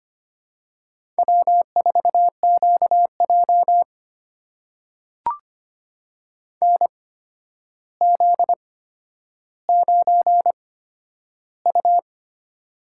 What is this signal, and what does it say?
Morse code, heard as a keyed tone, about 25 wpm, spaced as for 11 wpm, 700 Hz. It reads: W4QJ NZ9U
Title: USA Call Signs - 8x Character Spacing - 25wpm